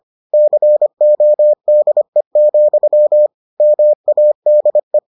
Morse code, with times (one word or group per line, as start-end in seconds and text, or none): 0.3-3.3 CODE,
3.6-5.0 MADE